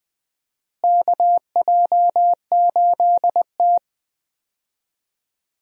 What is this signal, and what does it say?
Morse code, heard as a keyed tone, about 21 wpm, 700 Hz. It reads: KJ8T
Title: USA Call Signs - 4 Characters - Repeated 2x - 20wpm